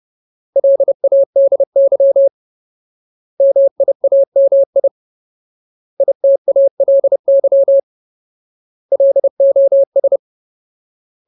Morse code, heard as a keyed tone, about 30 wpm, 550 Hz. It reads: LADY MIAMI ITALY LOS
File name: Ultimate Word List / 30WPM Ultimate